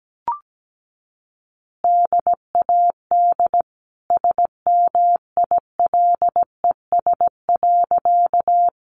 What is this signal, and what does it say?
Morse code, heard as a keyed tone, about 17 wpm, 700 Hz. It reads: DAD SMILES.
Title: Easy Phrases 17wpm